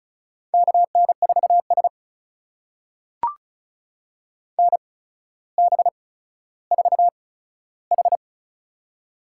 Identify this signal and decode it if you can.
Morse code, heard as a keyed tone, about 36 wpm, spaced as for 15 wpm, 700 Hz. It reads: KN4S NB4H
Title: USA Call Signs - 4 Characters - 8x Character Spacing - 35wpm